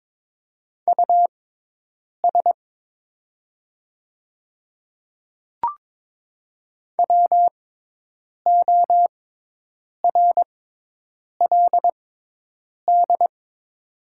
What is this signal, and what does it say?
Morse code, heard as a keyed tone, about 22 wpm, spaced as for 8 wpm, 700 Hz. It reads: US WORLD
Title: 6x Character Spacing - Top 200 Words - 22wpm